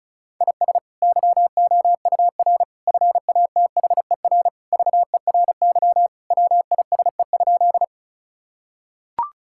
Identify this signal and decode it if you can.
Morse code, heard as a keyed tone, about 36 wpm, 700 Hz. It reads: IS YOUR FATHER VERY WISE?
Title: Sentences from Top 1600 Words 35wpm